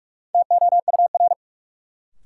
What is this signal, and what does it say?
Morse code, heard as a keyed tone, about 46 wpm, 700 Hz. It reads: TOUR
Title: Sentences from Top 1400 Words - New Word Drill Alt - 45wpm